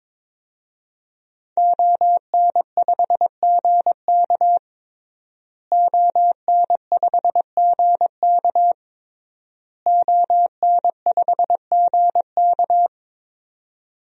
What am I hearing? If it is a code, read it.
Morse code, heard as a keyed tone, about 21 wpm, 700 Hz. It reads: ON5GK ON5GK ON5GK
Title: DXCC-100 Call Signs - 5 Characters - Repeated 3x - 22wpm